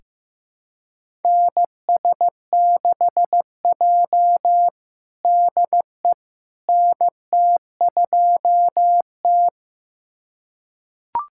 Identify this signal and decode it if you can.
Morse code, heard as a keyed tone, about 16 wpm, 700 Hz. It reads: NS6J DE NT2T
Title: USA Callsign DE Callsign - 4 Characters - 15wpm